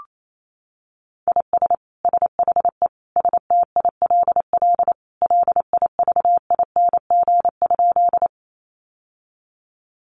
Morse code, heard as a keyed tone, about 28 wpm, 700 Hz. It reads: IS SHE STILL LIVING?
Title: Sentences from Top 1100 Words 28wpm